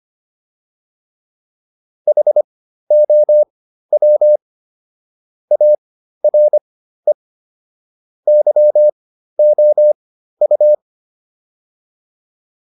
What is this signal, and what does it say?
Morse code, heard as a keyed tone, about 25 wpm, spaced as for 13 wpm, 600 Hz. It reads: HOW ARE YOU